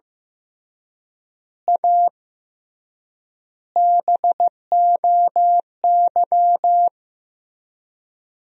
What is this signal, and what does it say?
Morse code, heard as a keyed tone, about 15 wpm, 700 Hz. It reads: A BOY
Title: 3x Word-Spacing - Sentences from Top 400 Words - 15wpm